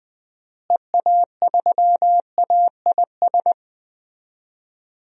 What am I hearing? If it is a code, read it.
Morse code, heard as a keyed tone, about 20 wpm, 700 Hz. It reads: EA3AIS
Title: DXCC-100 Call Signs - 6 Characters - Repeated 2x - 20wpm